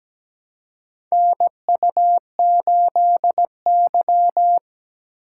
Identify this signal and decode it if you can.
Morse code, heard as a keyed tone, about 17 wpm, 700 Hz. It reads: NU8Y
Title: USA Call Signs - 4 Characters - 2x Character Spacing - 17wpm